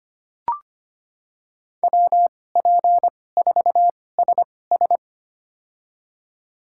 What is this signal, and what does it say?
Morse code, heard as a keyed tone, about 25 wpm, spaced as for 18 wpm, 700 Hz. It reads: WP4SS
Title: USA Call Signs - 5 Characters - 2x Character Spacing - 25wpm